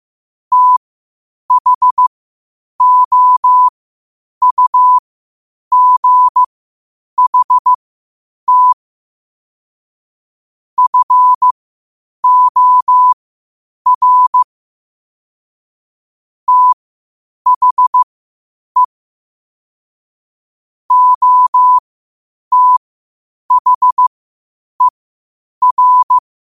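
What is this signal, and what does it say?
Morse code, heard as a keyed tone, about 15 wpm, spaced as for 8 wpm, 1000 Hz. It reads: THOUGHT FOR THE OTHER
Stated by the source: Morse code quotes at 8 WPM.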